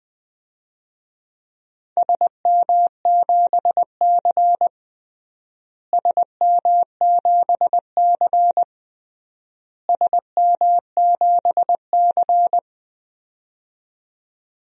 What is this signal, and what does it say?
Morse code, heard as a keyed tone, about 21 wpm, 700 Hz. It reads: SM7C SM7C SM7C
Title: DXCC-100 Call Signs - 4 Characters - Repeated 3x - 20wpm